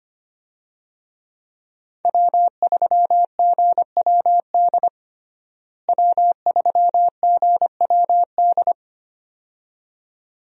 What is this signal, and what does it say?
Morse code, heard as a keyed tone, about 25 wpm, 700 Hz. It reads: W3GWD W3GWD
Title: USA Call Signs - 5 Characters - Repeated 2x - 25wpm